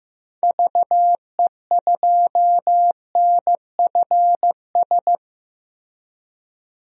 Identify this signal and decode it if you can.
Morse code, heard as a keyed tone, about 16 wpm, 700 Hz. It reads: VE2NFS
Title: DXCC-100 Call Signs - 6 Characters - Repeated 2x - 15wpm